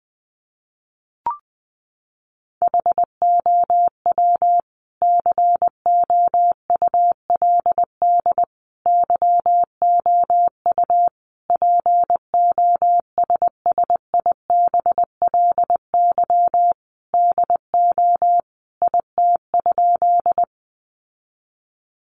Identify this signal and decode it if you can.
Morse code, heard as a keyed tone, about 20 wpm, 700 Hz. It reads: HOW COULD YOU POSSIBLY DO IT?
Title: Sentences from Top 1300 Words 20wpm